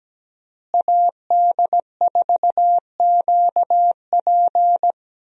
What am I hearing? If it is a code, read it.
Morse code, heard as a keyed tone, about 17 wpm, 700 Hz. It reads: AD4QP
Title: USA Call Signs - 5 Characters - Repeated 3x - 17wpm